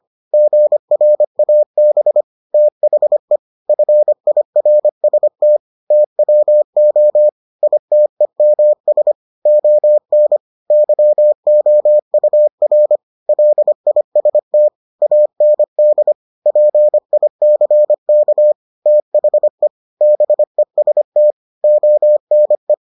Morse code, heard as a keyed tone, about 25 wpm, 600 Hz. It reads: GRAB THE FIRST TWO ITEMS ON YOUR LIST AND PICK THE BEST ONE